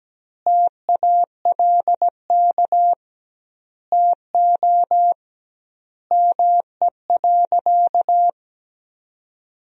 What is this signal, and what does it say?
Morse code, heard as a keyed tone, about 17 wpm, 700 Hz. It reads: TALK TO ME.
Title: 2x Word-Spacing - Sentences from Top 400 Words - 17wpm